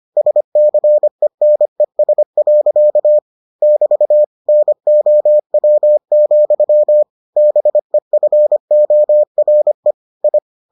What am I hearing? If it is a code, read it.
Morse code, heard as a keyed tone, about 25 wpm, 600 Hz. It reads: SCENES. <BT> NOW, BEFORE I